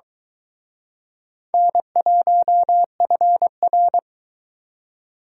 Morse code, heard as a keyed tone, about 23 wpm, 700 Hz. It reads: N1FR